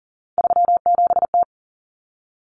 Morse code, heard as a keyed tone, about 40 wpm, 700 Hz. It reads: 37T
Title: Common Cut Numbers - 3-Digit 40wpm